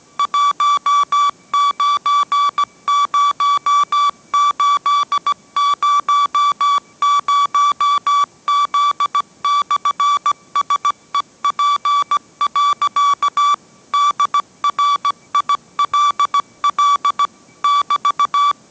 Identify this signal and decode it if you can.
Morse code, heard as a keyed tone, about 17 wpm, 1150 Hz. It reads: 190800Z/SEP. DRILL <BT>